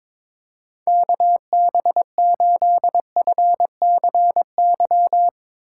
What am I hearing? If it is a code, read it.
Morse code, heard as a keyed tone, about 21 wpm, 700 Hz. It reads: KB8FCY
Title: USA Call Signs - 6 Characters - 2x Character Spacing - 22wpm